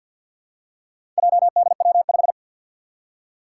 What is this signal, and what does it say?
Morse code, heard as a keyed tone, about 52 wpm, 700 Hz. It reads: JDW5
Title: Four Challenging Letter-Number Combinations 50wpm